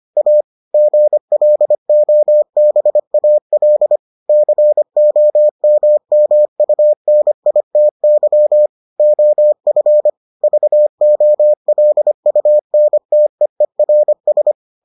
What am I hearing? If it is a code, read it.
Morse code, heard as a keyed tone, about 25 wpm, 600 Hz. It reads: A GLOBAL COMMUNITY OF VOLUNTEERS